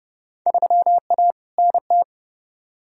Morse code, heard as a keyed tone, about 30 wpm, 700 Hz. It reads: 3A NT
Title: Field Day Exchanges Encore - 30wpm